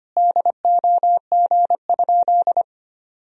Morse code, heard as a keyed tone, about 25 wpm, 700 Hz. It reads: DOG?
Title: Sentences from Top 1000 Words - Review Part 2 - 25wpm